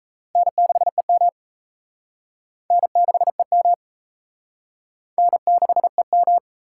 Morse code, heard as a keyed tone, about 36 wpm, 700 Hz. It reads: N6EM N6EM N6EM